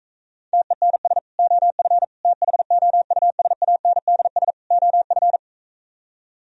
Morse code, heard as a keyed tone, about 44 wpm, 700 Hz. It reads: TENS OF THOUSANDS OF